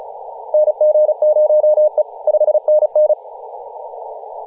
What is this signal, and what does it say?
Morse code, heard as a keyed tone, about 34 wpm, 610 Hz, with radio noise. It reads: NG0E 5NN
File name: posts / 2017-nov-04-2233z-vk9ma-21-025mhz-cw